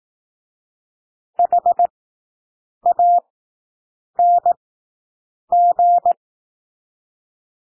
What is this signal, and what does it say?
Morse code, heard as a keyed tone, about 18 wpm, spaced as for 7 wpm, 700 Hz. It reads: HANG
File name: SOUND WORDS 18 wpm